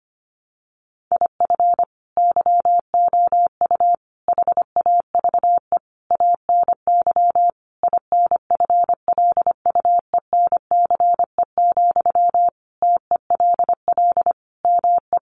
Morse code, heard as a keyed tone, about 25 wpm, 700 Hz. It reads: IF YOU HAVE ANY INFLUENCE, TELL ME
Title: Sentences from Top 1100 Words 25wpm